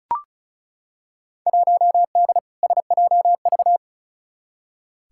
Morse code, heard as a keyed tone, about 36 wpm, 700 Hz. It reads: 1D SJV
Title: Field Day Exchanges Encore - 35wpm